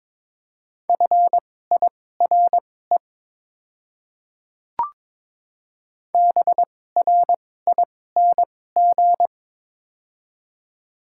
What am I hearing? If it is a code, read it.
Morse code, heard as a keyed tone, about 22 wpm, spaced as for 16 wpm, 700 Hz. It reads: FIRE BRING
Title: 2x Character Spacing - Top 500 Words - 22wpm